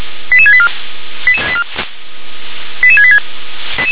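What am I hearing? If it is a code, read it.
Five-tone selective call (ZVEI), heard as quick runs of five tones, tones of about 70 ms. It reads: 88684 88686
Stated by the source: Five-tone ZVEI selective call 88770 received by SDR.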